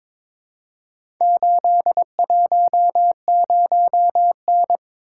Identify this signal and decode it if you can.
Morse code, heard as a keyed tone, about 22 wpm, 700 Hz. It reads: 810N
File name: practice / 22wpm / Four Letter-Number Combinations - 22wpm